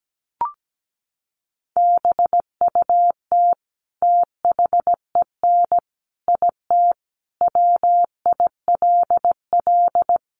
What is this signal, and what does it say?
Morse code, heard as a keyed tone, about 17 wpm, 700 Hz. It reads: BUT THEN IT WILL
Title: Sentences from Top 300 Words 17wpm